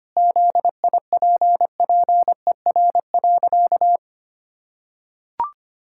Morse code, heard as a keyed tone, about 25 wpm, 700 Hz. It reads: ZIPPER.
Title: Pangram Challenge v2 - 25wpm